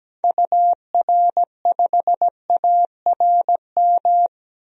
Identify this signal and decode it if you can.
Morse code, heard as a keyed tone, about 17 wpm, 700 Hz. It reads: UR5ARM
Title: DXCC-100 Call Signs - 6 Characters - 17wpm